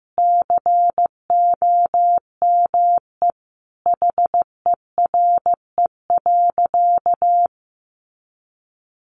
Morse code, heard as a keyed tone, about 15 wpm, 700 Hz. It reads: COME HERE.
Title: Sentences from Top 1500 Words 15wpm